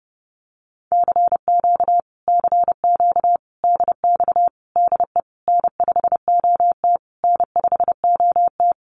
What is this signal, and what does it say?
Morse code, heard as a keyed tone, about 30 wpm, 700 Hz. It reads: CQ CQ DX DE N5OT N5OT
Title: Common QSO Phrases Encore 30wpm